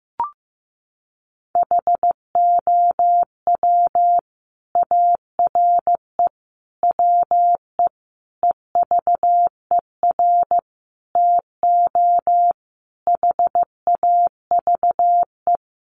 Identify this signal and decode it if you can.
Morse code, heard as a keyed tone, about 14 wpm, 700 Hz. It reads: HOW ARE WE EVER TO HAVE
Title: Sentences from Top 1500 Words 15wpm